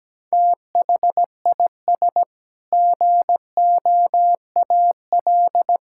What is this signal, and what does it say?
Morse code, heard as a keyed tone, about 17 wpm, 700 Hz. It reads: THIS GOAL
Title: Sentences from Top 1000 Words - Review Part 5 - 17wpm